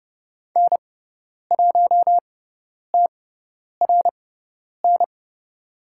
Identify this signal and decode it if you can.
Morse code, heard as a keyed tone, about 30 wpm, spaced as for 10 wpm, 700 Hz. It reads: N1TRN
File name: Lesson 16g - Letter c - Call Signs 30wpm